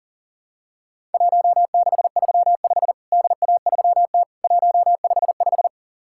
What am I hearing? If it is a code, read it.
Morse code, heard as a keyed tone, about 40 wpm, 700 Hz. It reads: 1635 DA3T 155
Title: Worked All Europe DX Contest - QTC Exchange - 40wpm